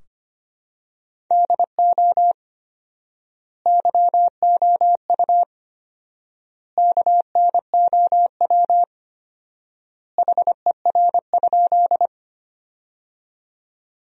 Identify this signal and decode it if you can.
Morse code, heard as a keyed tone, about 25 wpm, 700 Hz. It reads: DO YOU KNOW HER?
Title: 4x Word-Spacing - Sentences from Top 100 Words - 25wpm